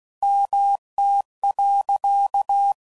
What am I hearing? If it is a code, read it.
Morse code, heard as a keyed tone, about 16 wpm, 800 Hz. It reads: MT.
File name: CW Code Groups @15WPM / fromSM's / Cities